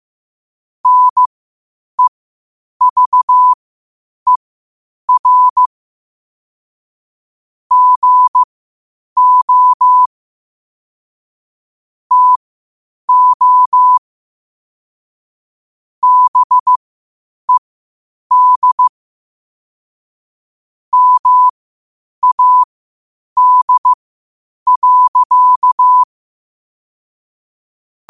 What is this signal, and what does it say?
Morse code, heard as a keyed tone, about 15 wpm, spaced as for 8 wpm, 1000 Hz. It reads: NEVER GO TO BED MAD.
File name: qotdfiles / 8wpm